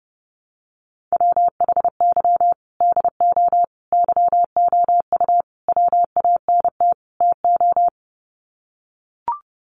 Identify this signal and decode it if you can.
Morse code, heard as a keyed tone, about 28 wpm, 700 Hz. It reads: WHY DO YOU WANT TO
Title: Sets of 5 Words Encore2 30wpm